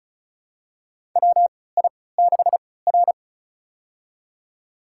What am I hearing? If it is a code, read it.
Morse code, heard as a keyed tone, about 36 wpm, spaced as for 20 wpm, 700 Hz. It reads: WI6R
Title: USA Call Signs - 4 Characters - 3x Character Spacing - 35wpm